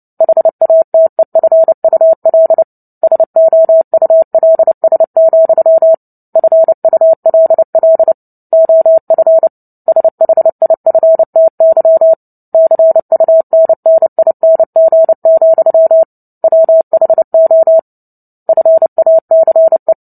Morse code, heard as a keyed tone, about 30 wpm, 650 Hz. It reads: HATEFUL SOULS, FULL OF SHIFTY CUNNING, WHO FACE
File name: Book Two / 03-Chapter 3 - 29 WPM